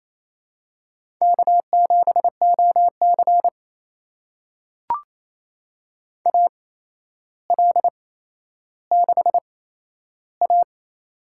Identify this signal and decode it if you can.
Morse code, heard as a keyed tone, about 28 wpm, spaced as for 12 wpm, 700 Hz. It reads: K7OC AL6A
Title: USA Call Signs - 4 Characters - 8x Character Spacing - 28wpm